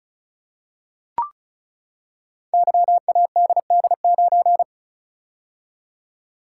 Morse code, heard as a keyed tone, about 36 wpm, 700 Hz. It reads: YADD9
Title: Code Groups - 5 Letter:Numbers - 35wpm